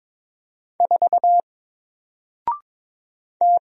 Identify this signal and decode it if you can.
Morse code, heard as a keyed tone, about 22 wpm, 700 Hz. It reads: 4 T
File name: Single-Letter-Number-22wpm